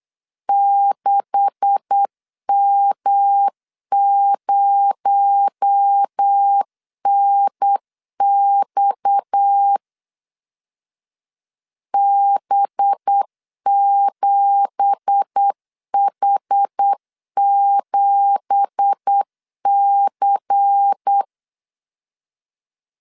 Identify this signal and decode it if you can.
Morse code, heard as a keyed tone, about 8 wpm, 800 Hz. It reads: 6M0NX B7H7C
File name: LEZIONI / 9 2867 8wpm 3440